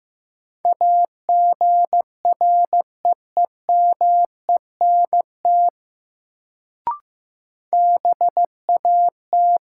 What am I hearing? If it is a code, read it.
Morse code, heard as a keyed tone, about 15 wpm, 700 Hz. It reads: AGREEMENT BAT
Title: Sentences from Top 1500 Words - New Word Drill - 15wpm